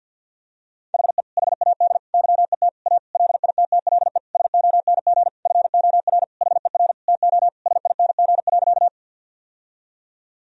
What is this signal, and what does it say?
Morse code, heard as a keyed tone, about 48 wpm, 700 Hz. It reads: HE HAD YET A LITTLE SONG FOR HER TO SING.
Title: Sentences from Top 1000 Words - Review Part 2 - 50wpm